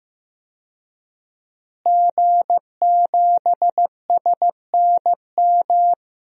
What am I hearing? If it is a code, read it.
Morse code, heard as a keyed tone, about 15 wpm, 700 Hz. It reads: G7SNM
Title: DXCC-100 Call Signs - 5 Characters - Repeated 2x - 15wpm